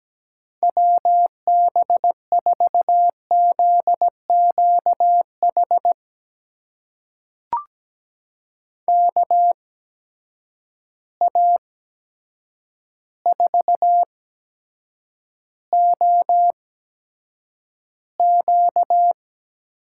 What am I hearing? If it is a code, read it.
Morse code, heard as a keyed tone, about 17 wpm, spaced as for 8 wpm, 700 Hz. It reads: WB4ZQH KA4OQ
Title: USA Call Signs - 6 Characters - 8x Character Spacing - 17wpm